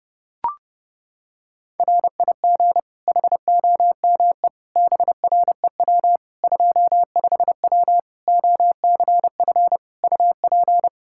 Morse code, heard as a keyed tone, about 30 wpm, 700 Hz. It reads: RIG HOME BREW 25W OCF UP
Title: Practice QSO - Exchange 2 - Info - 30wpm